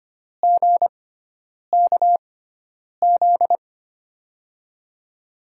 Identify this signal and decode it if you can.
Morse code, heard as a keyed tone, about 25 wpm, spaced as for 9 wpm, 700 Hz. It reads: GKZ
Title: USA Call Signs - 6 Characters - 6x Character Spacing - 25wpm